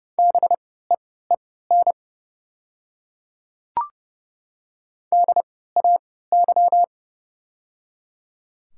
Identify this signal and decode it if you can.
Morse code, heard as a keyed tone, about 29 wpm, spaced as for 17 wpm, 700 Hz. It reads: BEEN DAY
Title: 3x Character Spacing - Top 100 Words - 30wpm